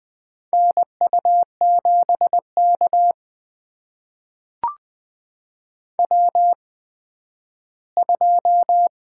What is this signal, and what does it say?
Morse code, heard as a keyed tone, about 20 wpm, spaced as for 12 wpm, 700 Hz. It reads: NU7K W2
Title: USA Call Signs - 4 Characters - 8x Character Spacing - 20wpm